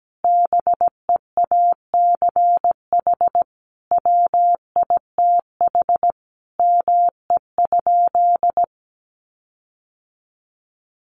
Morse code, heard as a keyed tone, about 17 wpm, 700 Hz. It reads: BEACH WITH ME?
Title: Sentences from Top 1500 Words 17wpm